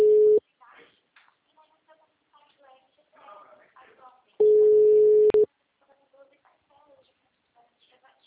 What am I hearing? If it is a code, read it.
Ringback tone (Europe, 425 Hz), with speech.